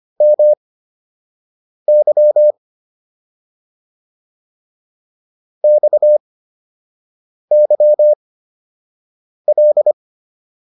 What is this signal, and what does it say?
Morse code, heard as a keyed tone, about 25 wpm, spaced as for 6 wpm, 600 Hz. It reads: MY XYL